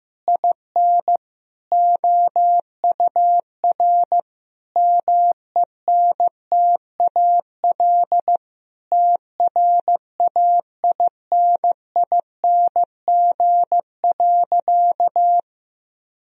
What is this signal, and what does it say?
Morse code, heard as a keyed tone, about 16 wpm, 700 Hz. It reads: IN OUR MENTAL TRAINING.
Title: Sentences from Top 1000 Words - Review Part 4 - 15wpm